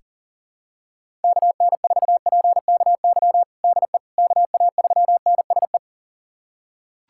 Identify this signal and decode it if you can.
Morse code, heard as a keyed tone, about 37 wpm, 700 Hz. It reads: KN4PKY DE KA3NSE
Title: USA Callsign DE Callsign - 6 Characters - 40wpm